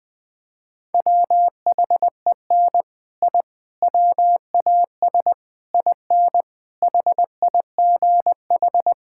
Morse code, heard as a keyed tone, about 21 wpm, 700 Hz. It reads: WHEN I WAS IN HIGH
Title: Sets of 6 Words - 20wpm